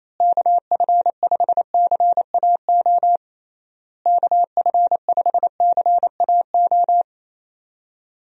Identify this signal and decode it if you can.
Morse code, heard as a keyed tone, about 28 wpm, 700 Hz. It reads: KF5CAO KF5CAO
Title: USA Call Signs - 6 Characters - Repeated 3x - 28wpm